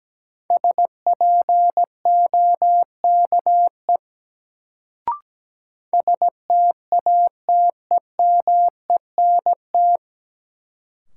Morse code, heard as a keyed tone, about 17 wpm, 700 Hz. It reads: SPOKE STATEMENT
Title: Sentences from Top 1200 Words - New Word Drill - 17wpm